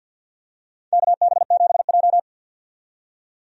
Morse code, heard as a keyed tone, about 52 wpm, 700 Hz. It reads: KB7J
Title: USA Call Signs - 4 Characters - 50wpm